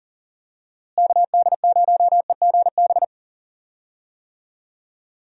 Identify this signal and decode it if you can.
Morse code, heard as a keyed tone, about 40 wpm, 700 Hz. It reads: KD0EGB
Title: USA Call Signs - 6 Characters - Repeated 2x - 40wpm